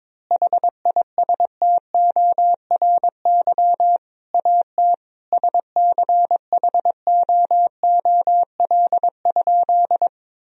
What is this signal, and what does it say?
Morse code, heard as a keyed tone, about 22 wpm, 700 Hz. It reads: HISTORY AT SCHOOL?
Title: Sentences from Top 1000 Words - Review Part 4 - 22wpm